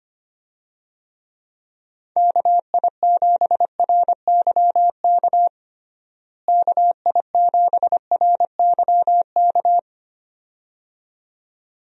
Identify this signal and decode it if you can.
Morse code, heard as a keyed tone, about 25 wpm, 700 Hz. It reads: KI7RYK KI7RYK
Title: USA Call Signs - 6 Characters - Repeated 2x - 25wpm